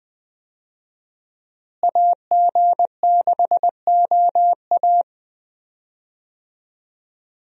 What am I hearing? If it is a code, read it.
Morse code, heard as a keyed tone, about 20 wpm, 700 Hz. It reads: AG6OA
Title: USA Call Signs - 5 Characters - Repeated 2x - 20wpm